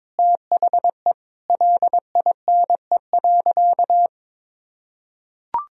Morse code, heard as a keyed tone, about 21 wpm, 700 Hz. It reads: THE LINE.
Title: Sentences from Top 1000 Words - Review Part 1 - 22wpm